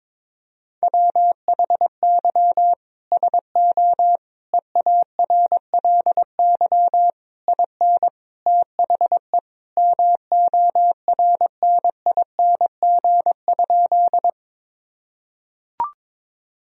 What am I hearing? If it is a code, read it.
Morse code, heard as a keyed tone, about 22 wpm, 700 Hz. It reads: WHY SO EARLY IN THE MORNING?
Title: Sentences from Top 1000 Words - Review Part 2 - 22wpm